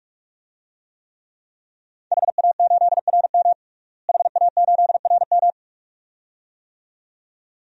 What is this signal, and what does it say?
Morse code, heard as a keyed tone, about 41 wpm, 700 Hz. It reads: HA8RM HA8RM
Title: DXCC-100 Call Signs - 5 Characters - Repeated 2x - 45wpm